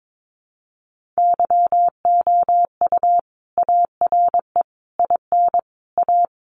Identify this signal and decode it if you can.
Morse code, heard as a keyed tone, about 22 wpm, 700 Hz. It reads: YOU ARE IN A
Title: Sentences from Top 1400 Words 22wpm